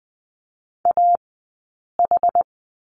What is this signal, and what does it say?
Morse code, heard as a keyed tone, about 20 wpm, 700 Hz. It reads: A H
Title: 073 Lesson 13d - Letter L - Two Characters 20wpm